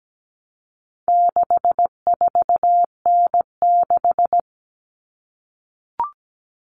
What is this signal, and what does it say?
Morse code, heard as a keyed tone, about 17 wpm, 700 Hz. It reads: 64N6
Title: All Cut Numbers - 4-Digit 17wpm